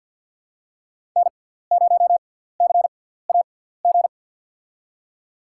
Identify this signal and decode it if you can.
Morse code, heard as a keyed tone, about 53 wpm, spaced as for 17 wpm, 700 Hz. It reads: N0CAG
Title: USA Call Signs - 5 Characters - 6x Character Spacing - 50wpm